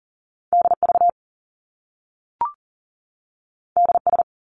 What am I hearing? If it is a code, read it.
Morse code, heard as a keyed tone, about 39 wpm, 700 Hz. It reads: DV DS
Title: Top Two Letter Combinations in Words 40wpm